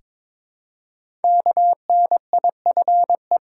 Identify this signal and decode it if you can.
Morse code, heard as a keyed tone, about 22 wpm, 700 Hz. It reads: KNIFE
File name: 5 Letter Words - First 3 Letters Given - 22wpm